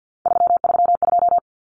Morse code, heard as a keyed tone, about 50 wpm, 700 Hz. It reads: 332
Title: Numbers - 3-Digit 50wpm